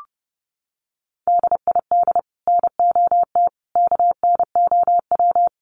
Morse code, heard as a keyed tone, about 30 wpm, 700 Hz. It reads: DID NOT KNOW
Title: Sets of 5 Words Encore2 30wpm